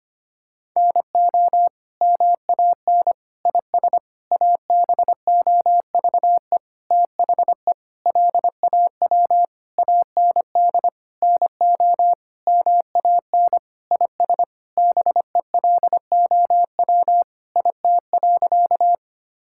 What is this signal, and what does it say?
Morse code, heard as a keyed tone, about 25 wpm, 700 Hz. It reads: NO MAN IS ABOVE THE LAW AND NO MAN IS BELOW IT.
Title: Sentences from Top 1000 Words - Review Part 5 - 25wpm